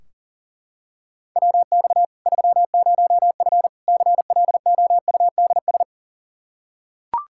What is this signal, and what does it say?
Morse code, heard as a keyed tone, about 40 wpm, 700 Hz. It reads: WX 30F CLOUDS
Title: Practice QSO - Exchange 2 - Info - 40wpm